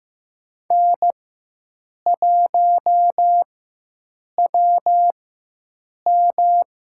Morse code, heard as a keyed tone, about 15 wpm, spaced as for 7 wpm, 700 Hz. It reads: N1WM
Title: USA Call Signs - 4x Character Spacing - 15wpm